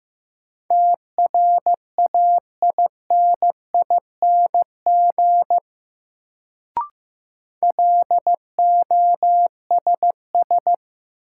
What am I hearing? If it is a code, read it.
Morse code, heard as a keyed tone, about 15 wpm, 700 Hz. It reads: TRAINING LOSS
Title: Sentences from Top 1400 Words - New Word Drill Alt - 15wpm